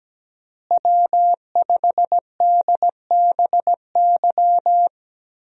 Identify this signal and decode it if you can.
Morse code, heard as a keyed tone, about 17 wpm, 700 Hz. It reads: W5DBY
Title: USA Call Signs - 5 Characters - Repeated 2x - 17wpm